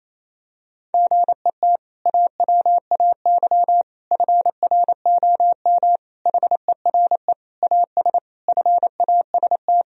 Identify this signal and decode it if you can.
Morse code, heard as a keyed tone, about 27 wpm, 700 Hz. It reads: GET AWAY FROM HERE AS FAST
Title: Sentences from Top 1000 Words - Review Part 2 - 28wpm